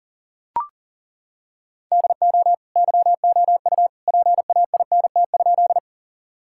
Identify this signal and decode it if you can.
Morse code, heard as a keyed tone, about 37 wpm, 700 Hz. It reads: DO YOU PAINT?
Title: Sentences from Top 1000 Words - Review Part 6 - 40wpm